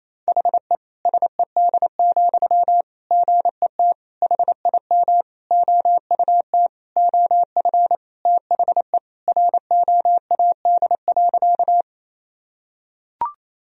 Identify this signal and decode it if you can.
Morse code, heard as a keyed tone, about 28 wpm, 700 Hz. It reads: HE SED, GET HIM OUT OF THE ROAD.
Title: Sentences from Top 400 Words and Abbreviations 28wpm